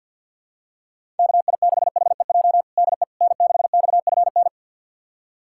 Morse code, heard as a keyed tone, about 48 wpm, 700 Hz. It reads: KI6HEJ DE N6XFN
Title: USA Callsign DE Callsign 50wpm